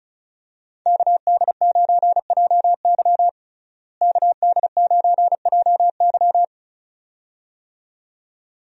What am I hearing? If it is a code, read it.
Morse code, heard as a keyed tone, about 34 wpm, 700 Hz. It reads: KD9JY KD9JY
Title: USA Call Signs - 5 Characters - Repeated 2x - 35wpm